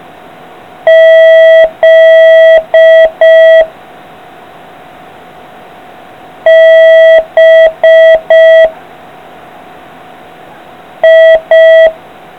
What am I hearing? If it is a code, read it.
Morse code, heard as a keyed tone, about 5 wpm, 660 Hz. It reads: Z B I